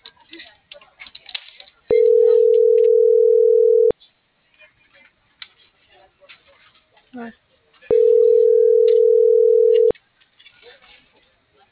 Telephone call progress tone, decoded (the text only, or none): ringback tone